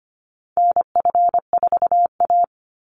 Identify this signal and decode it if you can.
Morse code, heard as a keyed tone, about 25 wpm, 700 Hz. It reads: NF4A
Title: Call Signs - 25wpm